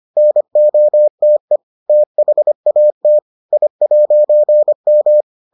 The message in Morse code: NOTE THAT I'M

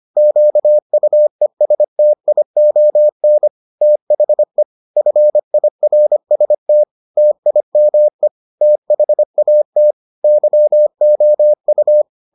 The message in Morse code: QUESTION THE FIRST TIME THAT YOU

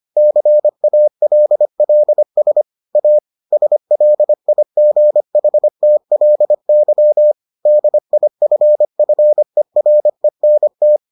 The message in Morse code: CALLS A SLIGHTLY DIFFERENT

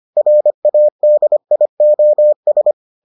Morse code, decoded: RADIOS

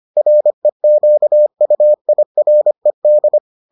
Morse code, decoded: REQUIRED

